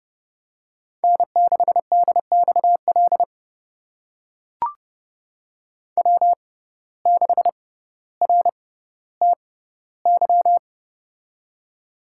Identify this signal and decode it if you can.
Morse code, heard as a keyed tone, about 32 wpm, spaced as for 16 wpm, 700 Hz. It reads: N6DXL W6RTY
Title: USA Call Signs - 5 Characters - 6x Character Spacing - 30wpm